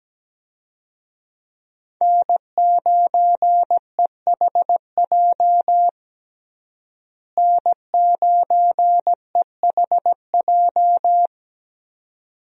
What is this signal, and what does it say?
Morse code, heard as a keyed tone, about 17 wpm, 700 Hz. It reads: N9EHJ N9EHJ
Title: USA Call Signs - 5 Characters - Repeated 2x - 17wpm